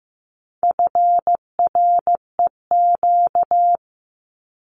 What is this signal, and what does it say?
Morse code, heard as a keyed tone, about 15 wpm, 700 Hz. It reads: FREQ
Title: Abbreviations 15wpm